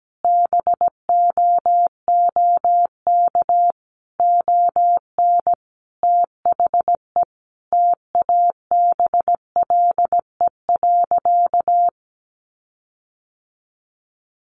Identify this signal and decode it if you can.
Morse code, heard as a keyed tone, about 17 wpm, 700 Hz. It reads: BOOK ON THE TABLE.
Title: Sentences from Top 1400 Words 17wpm